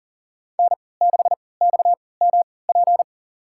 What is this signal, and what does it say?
Morse code, heard as a keyed tone, about 40 wpm, spaced as for 23 wpm, 700 Hz. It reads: N6XMP
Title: USA Call Signs - 5 Characters - 3x Character Spacing - 40wpm